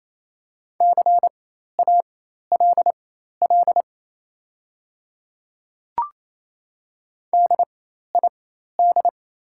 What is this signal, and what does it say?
Morse code, heard as a keyed tone, about 28 wpm, spaced as for 13 wpm, 700 Hz. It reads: CALL DID
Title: 4x Character Spacing - Top 200 Words - 28wpm